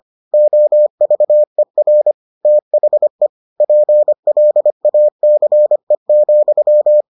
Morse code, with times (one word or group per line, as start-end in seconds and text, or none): 0.3-2.1 OVER
2.5-3.3 THE
3.6-7.0 PLACE,